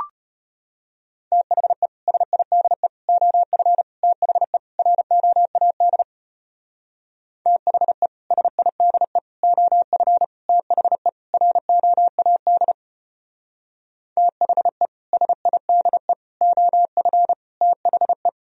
Morse code, heard as a keyed tone, about 31 wpm, 700 Hz. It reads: THE SIDE OF THE ROAD THE SIDE OF THE ROAD THE SIDE OF THE
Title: Speed-Racing - Sets of 5 Words 25wpm